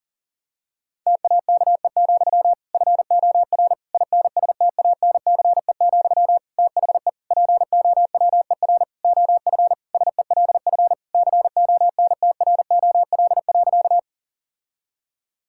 Morse code, heard as a keyed tone, about 40 wpm, 700 Hz. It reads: TAKE, FOR INSTANCE, THE POWER OF SELF CONTROL.
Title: Sentences from Top 1000 Words - Review Part 5 - 40wpm